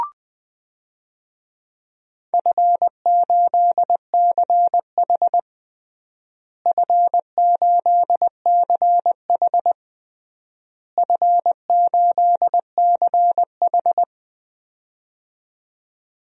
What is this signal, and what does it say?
Morse code, heard as a keyed tone, about 21 wpm, 700 Hz. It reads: F8CH F8CH F8CH
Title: DXCC-100 Call Signs - 4 Characters - Repeated 3x - 20wpm